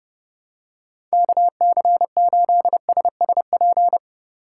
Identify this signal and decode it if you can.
Morse code, heard as a keyed tone, about 32 wpm, 700 Hz. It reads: KC8SSP
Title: USA Call Signs - 8x Character Spacing - 30wpm